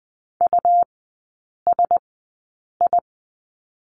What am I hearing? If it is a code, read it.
Morse code, heard as a keyed tone, about 20 wpm, spaced as for 8 wpm, 700 Hz. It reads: USI
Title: 107 Lesson 18e - Letter m - Three Characters 20wpm